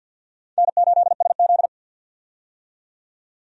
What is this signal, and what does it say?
Morse code, heard as a keyed tone, about 52 wpm, 700 Hz. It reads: N8SZ
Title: USA Call Signs - 4 Characters - 50wpm